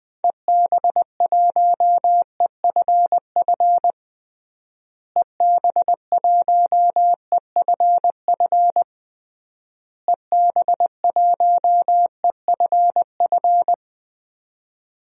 Morse code, heard as a keyed tone, about 21 wpm, 700 Hz. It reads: EB1EFF EB1EFF EB1EFF
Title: DXCC-100 Call Signs - 6 Characters - Repeated 3x - 20wpm